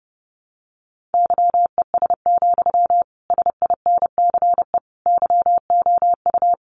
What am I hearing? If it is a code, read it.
Morse code, heard as a keyed tone, about 28 wpm, 700 Hz. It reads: YES, SINCE YOU
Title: Sentences from Top 1300 Words 30wpm